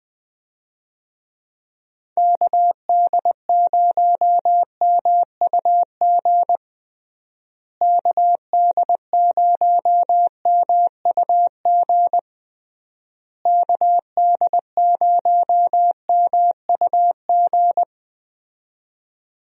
Morse code, heard as a keyed tone, about 20 wpm, 700 Hz. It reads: KD0MUG KD0MUG KD0MUG
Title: USA Call Signs - 6 Characters - Repeated 3x - 20wpm